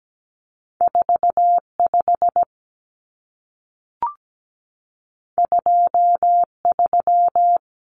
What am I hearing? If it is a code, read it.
Morse code, heard as a keyed tone, about 17 wpm, 700 Hz. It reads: 45 23
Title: Numbers - 2-Digit 17wpm